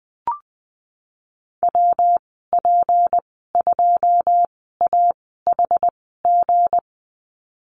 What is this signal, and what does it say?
Morse code, heard as a keyed tone, about 20 wpm, spaced as for 14 wpm, 700 Hz. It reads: WP2AHG
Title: 161 Lesson 26g - Letter g - Call Signs 20wpm